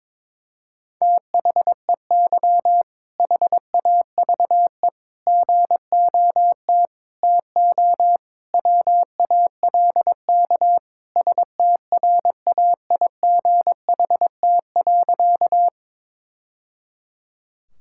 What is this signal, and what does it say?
Morse code, heard as a keyed tone, about 22 wpm, 700 Hz. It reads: THEY HAVE GOT TO WALK STRAIGHT.
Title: Sentences from Top 1000 Words - Review Part 6 - 22wpm